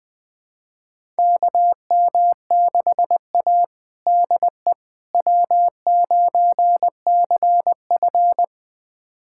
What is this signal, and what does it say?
Morse code, heard as a keyed tone, about 20 wpm, 700 Hz. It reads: KM6A DE W9CF
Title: USA Callsign DE Callsign - 4 Characters - 20wpm